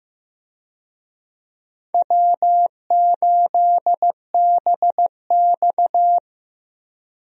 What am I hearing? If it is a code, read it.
Morse code, heard as a keyed tone, about 15 wpm, 700 Hz. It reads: W8BX
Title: USA Call Signs - 4 Characters - Repeated 3x - 15wpm